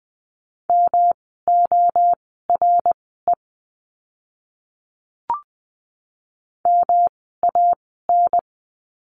Morse code, heard as a keyed tone, about 20 wpm, spaced as for 14 wpm, 700 Hz. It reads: MORE MAN
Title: 108 Lesson 18f - Letter m - Words 20wpm